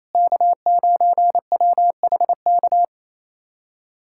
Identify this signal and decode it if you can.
Morse code, heard as a keyed tone, about 28 wpm, 700 Hz. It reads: K9WHK
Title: USA Callsign DE Callsign - 5 Characters - 28wpm